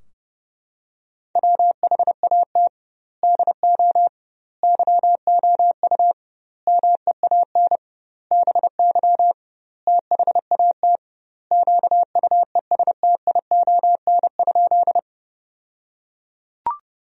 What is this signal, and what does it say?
Morse code, heard as a keyed tone, about 30 wpm, 700 Hz. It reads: WHAT DO YOU MEAN BY THAT QUESTION?
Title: 2x Word-Spacing - Sentences from Top 400 Words - 30wpm